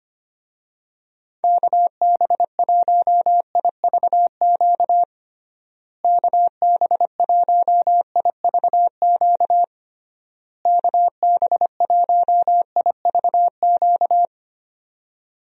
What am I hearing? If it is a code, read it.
Morse code, heard as a keyed tone, about 25 wpm, 700 Hz. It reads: KB1IVQ KB1IVQ KB1IVQ
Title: USA Call Signs - 6 Characters - Repeated 3x - 25wpm